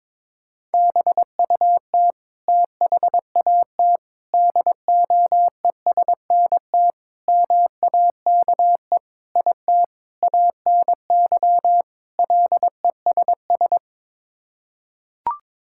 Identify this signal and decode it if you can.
Morse code, heard as a keyed tone, about 22 wpm, 700 Hz. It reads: BUT THAT DOESNT MAKE IT ANY LESS
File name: practice / 22wpm / Sets of 7 Words - 22wpm